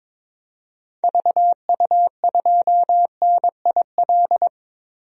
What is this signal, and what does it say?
Morse code, heard as a keyed tone, about 22 wpm, 700 Hz. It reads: VU2NIL
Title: DXCC-100 Call Signs - 6 Characters - Repeated 2x - 22wpm